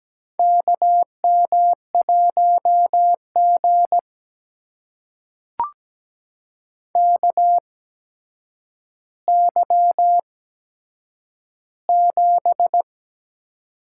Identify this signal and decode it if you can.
Morse code, heard as a keyed tone, about 17 wpm, spaced as for 8 wpm, 700 Hz. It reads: KM1G KY7
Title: USA Call Signs - 4 Characters - 8x Character Spacing - 17wpm